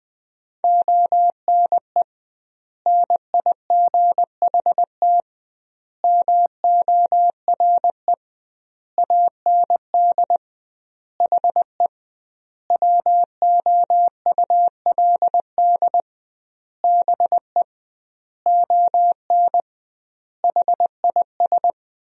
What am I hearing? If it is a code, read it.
Morse code, heard as a keyed tone, about 20 wpm, 700 Hz. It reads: ONE NIGHT MORE AND HE WOULD BE ON HIS
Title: 2x Word-Spacing - Sentences from Top 300 Words - 20wpm